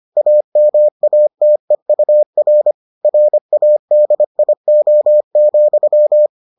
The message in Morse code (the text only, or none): AMATEUR RADIO,